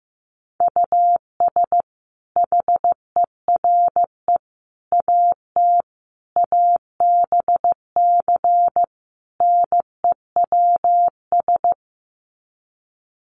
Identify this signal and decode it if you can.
Morse code, heard as a keyed tone, about 15 wpm, 700 Hz. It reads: US HERE AT ABC NEWS
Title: Sets of 5 Words 15wpm